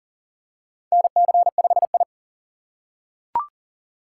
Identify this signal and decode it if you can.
Morse code, heard as a keyed tone, about 37 wpm, 700 Hz. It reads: NC5I
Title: USA Call Signs - 4 Characters - 6x Character Spacing - 40wpm